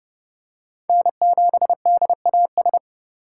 Morse code, heard as a keyed tone, about 30 wpm, 700 Hz. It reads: N7DAS
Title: USA Call Signs - 2x Character Spacing - 30wpm